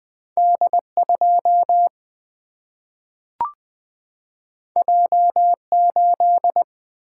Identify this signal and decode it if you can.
Morse code, heard as a keyed tone, about 20 wpm, 700 Hz. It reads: D 2 J 8
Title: Single-Letter Single-Number 20wpm